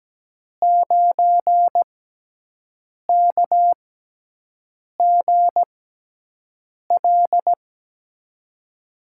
Morse code, heard as a keyed tone, about 17 wpm, spaced as for 6 wpm, 700 Hz. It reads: 9KGL
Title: USA Call Signs - 6x Character Spacing - 17wpm